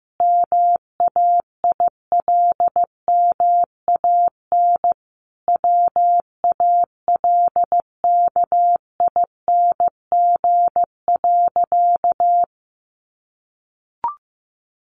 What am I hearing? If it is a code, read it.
Morse code, heard as a keyed tone, about 14 wpm, 700 Hz. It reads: MAILMAN WALKING.
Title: Easy Phrases 15wpm